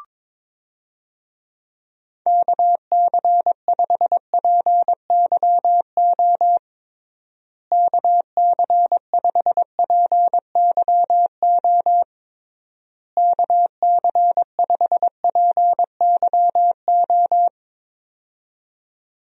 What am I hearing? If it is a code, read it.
Morse code, heard as a keyed tone, about 22 wpm, 700 Hz. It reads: KC5PYO KC5PYO KC5PYO
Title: USA Call Signs - 6 Characters - Repeated 3x - 22wpm